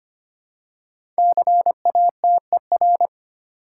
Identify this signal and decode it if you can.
Morse code, heard as a keyed tone, about 25 wpm, 700 Hz. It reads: CATER
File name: 5 Letter Words - First 3 Letters Given - 25wpm